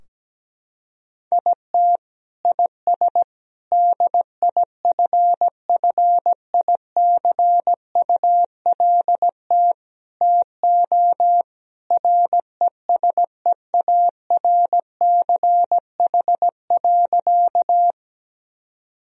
Morse code, heard as a keyed tone, about 17 wpm, 700 Hz. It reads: IT IS DIFFICULT TO RESEARCH.
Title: Sentences from Top 1000 Words - Review Part 4 - 17wpm